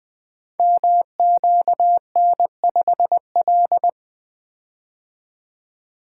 Morse code, heard as a keyed tone, about 20 wpm, 700 Hz. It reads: MQN5L
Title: Code Groups - 5 Letter:Numbers - 20wpm